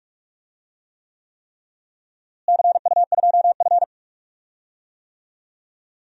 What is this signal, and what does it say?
Morse code, heard as a keyed tone, about 46 wpm, 700 Hz. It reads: CU2F
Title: DXCC-100 Call Signs - 4 Characters - Repeated 3x - 45wpm